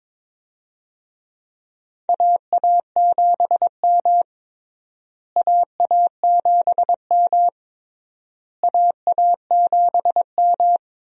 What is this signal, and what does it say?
Morse code, heard as a keyed tone, about 21 wpm, 700 Hz. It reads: AA7M AA7M AA7M
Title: USA Call Signs - 4 Characters - Repeated 3x - 22wpm